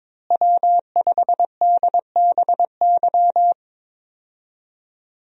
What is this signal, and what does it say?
Morse code, heard as a keyed tone, about 22 wpm, 700 Hz. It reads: W5DBY
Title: USA Call Signs - 5 Characters - Repeated 2x - 22wpm